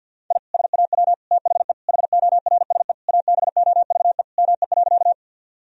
Morse code, heard as a keyed tone, about 53 wpm, 700 Hz. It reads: I SAW THE HORSE ABOVE ME.
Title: Sentences from Top 1000 Words - Review Part 2 - 50wpm